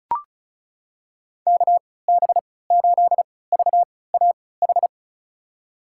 Morse code, heard as a keyed tone, about 36 wpm, spaced as for 20 wpm, 690 Hz. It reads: KB8VAH
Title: USA Call Signs - 3x Character Spacing - 35wpm